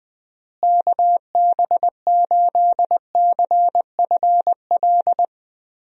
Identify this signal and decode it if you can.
Morse code, heard as a keyed tone, about 20 wpm, 700 Hz. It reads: KB8CFL
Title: USA Call Signs - 6 Characters - Repeated 2x - 20wpm